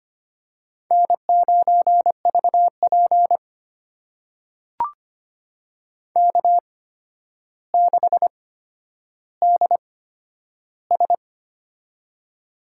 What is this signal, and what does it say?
Morse code, heard as a keyed tone, about 25 wpm, spaced as for 11 wpm, 700 Hz. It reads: N9VP K6DS
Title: USA Call Signs - 4 Characters - 8x Character Spacing - 25wpm